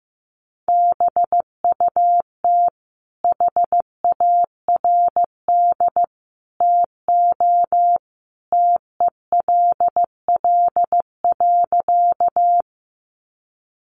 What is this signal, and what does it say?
Morse code, heard as a keyed tone, about 15 wpm, 700 Hz. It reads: BUT HARD TO TELL.
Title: Sentences from Top 300 Words 15wpm